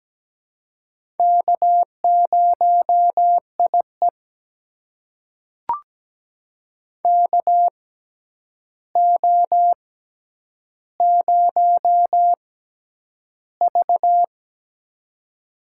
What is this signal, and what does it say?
Morse code, heard as a keyed tone, about 17 wpm, spaced as for 9 wpm, 700 Hz. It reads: K0IE KO0V
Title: USA Call Signs - 4 Characters - 6x Character Spacing - 17wpm